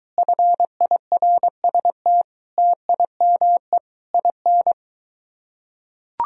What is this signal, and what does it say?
Morse code, heard as a keyed tone, about 23 wpm, 700 Hz. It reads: FIRST TIME IN